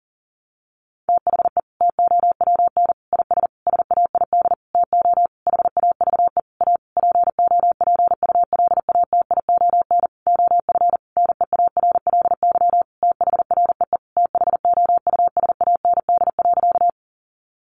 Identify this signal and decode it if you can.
Morse code, heard as a keyed tone, about 40 wpm, 700 Hz. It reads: THE TOWN IS SAID TO HAVE A POPULATION OF NEARLY THREE THOUSAND.
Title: Sentences from Top 1300 Words 40wpm